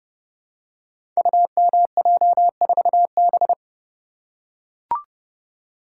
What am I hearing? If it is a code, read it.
Morse code, heard as a keyed tone, about 30 wpm, 700 Hz. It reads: UMJ4B
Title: Code Groups - 5 Letter:Numbers - 30wpm